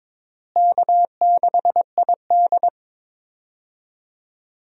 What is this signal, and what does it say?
Morse code, heard as a keyed tone, about 22 wpm, 700 Hz. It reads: K6ID
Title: USA Call Signs - 4 Characters - Repeated 2x - 22wpm